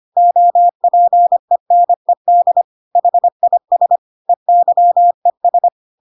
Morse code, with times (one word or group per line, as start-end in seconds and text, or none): 0.2-2.6 OPENED
3.0-4.0 HIS
4.3-5.7 EYES